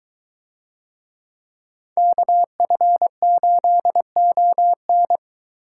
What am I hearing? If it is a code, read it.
Morse code, heard as a keyed tone, about 23 wpm, 700 Hz. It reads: KF8ON